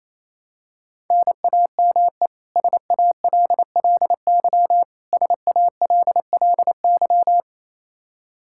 Morse code, heard as a keyed tone, about 28 wpm, 700 Hz. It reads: NAME SALLY SALLY
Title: K1USN SST QSO Practice 28wpm